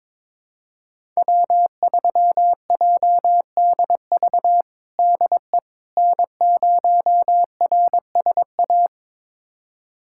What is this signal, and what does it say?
Morse code, heard as a keyed tone, about 22 wpm, 700 Hz. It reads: W3JDV DE N0RSA
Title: USA Callsign DE Callsign - 5 Characters - 22wpm